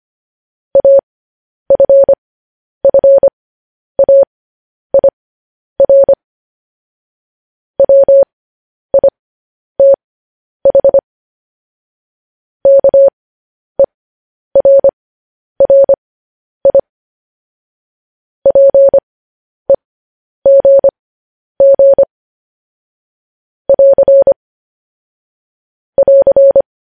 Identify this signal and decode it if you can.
Morse code, heard as a keyed tone, about 25 wpm, spaced as for 10 wpm, 550 Hz. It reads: AFFAIR WITH KERRI PEGG <AR> <AR>
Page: news headlines in morse code at 10 wpm